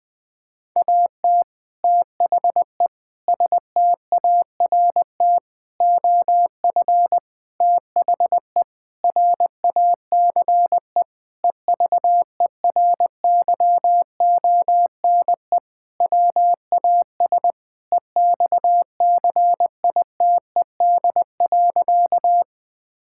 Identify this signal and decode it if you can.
Morse code, heard as a keyed tone, about 21 wpm, 700 Hz. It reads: AT THE START OF THE RACE EVERYONE WAS EXCITED.
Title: Sentences with 5-Word Phrases - 20wpm